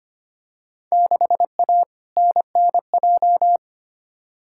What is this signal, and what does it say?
Morse code, heard as a keyed tone, about 25 wpm, 700 Hz. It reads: 6A NNJ
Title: Field Day Exchanges Encore - 25wpm